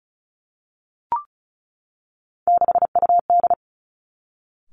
Morse code, heard as a keyed tone, about 36 wpm, 700 Hz. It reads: 6UD
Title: Three Challenging Letter-Number Combinations 35wpm